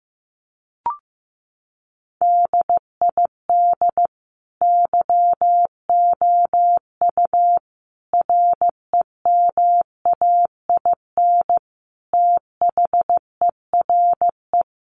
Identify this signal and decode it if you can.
Morse code, heard as a keyed tone, about 15 wpm, 700 Hz. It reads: DID YOU REMAIN THERE
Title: Sentences from Top 1200 Words 15wpm